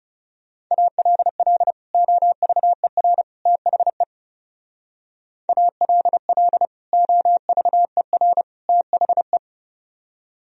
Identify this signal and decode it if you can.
Morse code, heard as a keyed tone, about 36 wpm, 700 Hz. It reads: ALL OVER THE ALL OVER THE
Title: Speed-Racing - Sets of 3 Words 30wpm